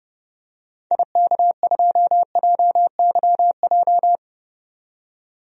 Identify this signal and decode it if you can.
Morse code, heard as a keyed tone, about 32 wpm, 700 Hz. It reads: IK2JYJ
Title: DXCC-100 Call Signs - 6 Characters - 30wpm